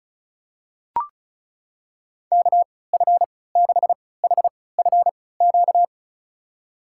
Morse code, heard as a keyed tone, about 34 wpm, spaced as for 20 wpm, 700 Hz. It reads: KF6HFQ
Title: USA Call Signs - 3x Character Spacing - 35wpm